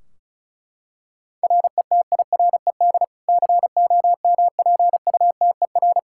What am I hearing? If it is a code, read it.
Morse code, heard as a keyed tone, about 36 wpm, 700 Hz. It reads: RETIRED COMPUTER
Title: Practice QSO - Exchange 3 - More Info - 35wpm